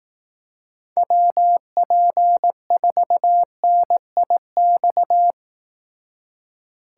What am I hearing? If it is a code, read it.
Morse code, heard as a keyed tone, about 18 wpm, 700 Hz. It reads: WP4NIX
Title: Speed Racing - USA 2x3 Call Signs 15wpm